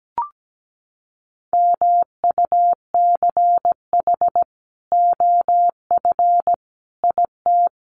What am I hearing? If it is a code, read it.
Morse code, heard as a keyed tone, about 17 wpm, 700 Hz. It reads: MUCH OF IT
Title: Sentences from Top 400 Words 17wpm